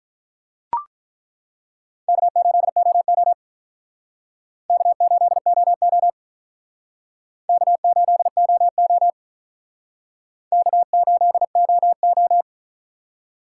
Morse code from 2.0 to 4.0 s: K8OO

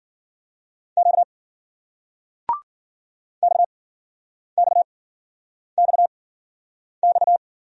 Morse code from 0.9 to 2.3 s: Q

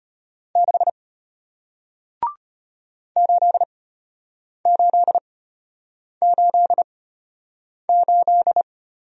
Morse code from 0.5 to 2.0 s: 6

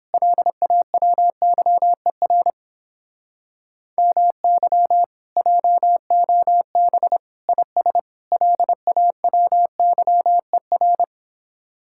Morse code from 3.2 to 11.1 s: MY JOB IS LAWYER

